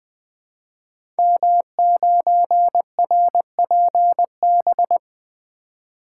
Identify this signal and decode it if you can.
Morse code, heard as a keyed tone, about 20 wpm, 700 Hz. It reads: M9RPB
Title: Code Groups - 5 Letter:Numbers - 20wpm